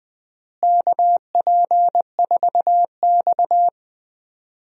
Morse code, heard as a keyed tone, about 20 wpm, 700 Hz. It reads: KP4X
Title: DXCC-100 Call Signs - 4 Characters - Repeated 3x - 20wpm